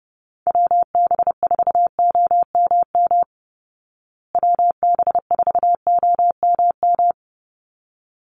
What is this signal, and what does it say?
Morse code, heard as a keyed tone, about 30 wpm, 700 Hz. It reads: WB4OMM WB4OMM
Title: Call Signs Repeated 3x - 30wpm